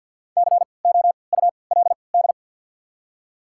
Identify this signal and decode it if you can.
Morse code, heard as a keyed tone, about 53 wpm, spaced as for 29 wpm, 700 Hz. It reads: COULD
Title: 3x Character Spacing - Top 200 Words - 50wpm